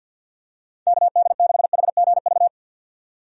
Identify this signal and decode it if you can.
Morse code, heard as a keyed tone, about 51 wpm, 700 Hz. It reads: KD6HGV
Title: USA Call Signs - 6 Characters - 50wpm